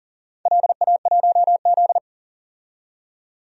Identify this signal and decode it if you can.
Morse code, heard as a keyed tone, about 40 wpm, 700 Hz. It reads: LA1Z
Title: DXCC-100 Call Signs - 4 Characters - 40wpm